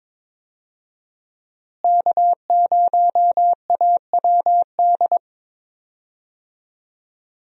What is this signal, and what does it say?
Morse code, heard as a keyed tone, about 22 wpm, 700 Hz. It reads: K0AWD
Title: USA Call Signs - 5 Characters - Repeated 2x - 22wpm